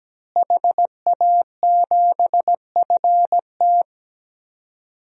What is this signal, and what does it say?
Morse code, heard as a keyed tone, about 17 wpm, 700 Hz. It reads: HA7FT
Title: DXCC-100 Call Signs - 5 Characters - Repeated 3x - 17wpm